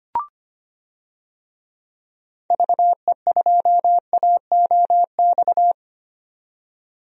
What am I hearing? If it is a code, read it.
Morse code, heard as a keyed tone, about 25 wpm, 700 Hz. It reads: VE2AOX
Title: DXCC-100 Call Signs - 6 Characters - 25wpm